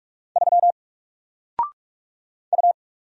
Morse code, heard as a keyed tone, about 44 wpm, 700 Hz. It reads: 3 U
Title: Single Character (Letter:Number:Punctuation:Prosign) 45wpm